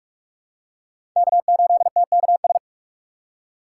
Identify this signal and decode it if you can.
Morse code, heard as a keyed tone, about 46 wpm, 700 Hz. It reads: K8TKS
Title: USA Call Signs - 5 Characters - 3x Character Spacing - 45wpm